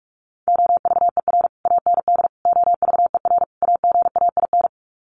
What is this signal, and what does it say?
Morse code, heard as a keyed tone, about 44 wpm, 700 Hz. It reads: OVER AND OVER AGAIN
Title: Sets of 4 Words 45wpm